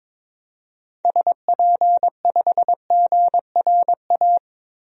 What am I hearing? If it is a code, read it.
Morse code, heard as a keyed tone, about 21 wpm, 700 Hz. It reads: SP5GRA
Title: DXCC-100 Call Signs - 6 Characters - Repeated 2x - 22wpm